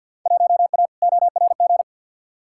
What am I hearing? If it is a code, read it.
Morse code, heard as a keyed tone, about 51 wpm, 700 Hz. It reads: 1A ORG
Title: Field Day Exchanges Encore - 50wpm